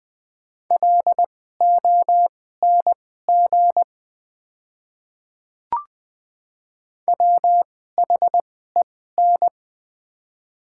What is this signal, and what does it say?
Morse code, heard as a keyed tone, about 20 wpm, spaced as for 14 wpm, 700 Hz. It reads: LONG WHEN
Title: 2x Character Spacing - Top 100 Words - 20wpm